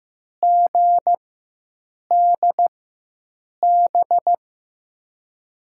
Morse code, heard as a keyed tone, about 15 wpm, spaced as for 7 wpm, 700 Hz. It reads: GDB
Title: USA Call Signs - 6 Characters - 4x Character Spacing - 15wpm